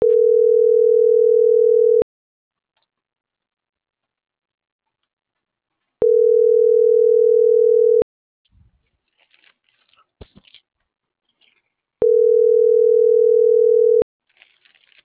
Ringback tone (North America), 440 + 480 Hz.
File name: CENTRAL MAERSK / out adrian.moreno